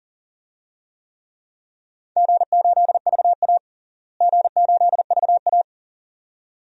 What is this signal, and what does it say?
Morse code, heard as a keyed tone, about 39 wpm, 700 Hz. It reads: G8VA G8VA
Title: DXCC-100 Call Signs - 4 Characters - Repeated 2x - 40wpm